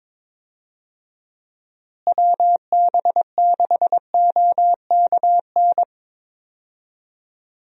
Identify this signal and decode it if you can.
Morse code, heard as a keyed tone, about 22 wpm, 700 Hz. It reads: WB6OKN
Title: USA Call Signs - 6 Characters - Repeated 3x - 22wpm